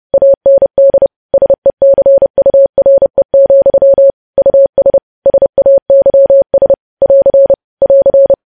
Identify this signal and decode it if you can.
Morse code, heard as a keyed tone, about 30 wpm, 550 Hz. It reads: AND SECURE, US SAYS <AR> <AR>
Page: news headlines in morse code at 30 wpm